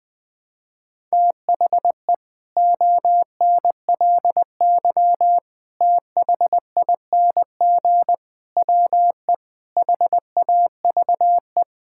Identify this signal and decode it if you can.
Morse code, heard as a keyed tone, about 20 wpm, 700 Hz. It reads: THE ONLY THING WE HAVE
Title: Sets of 8 Words - 20wpm